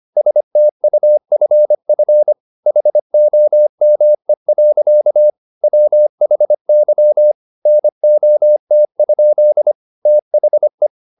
Morse code, heard as a keyed tone, about 25 wpm, 600 Hz. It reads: STUFF HOME. WHY NOT? THE